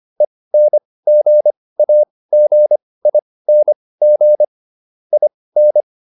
Morse code, heard as a keyed tone, about 25 wpm, spaced as for 18 wpm, 600 Hz. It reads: ENGAGING IN